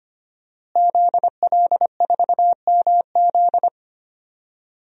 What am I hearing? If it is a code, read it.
Morse code, heard as a keyed tone, about 25 wpm, 700 Hz. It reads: ZL4MZ
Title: DXCC-100 Call Signs - 5 Characters - Repeated 2x - 25wpm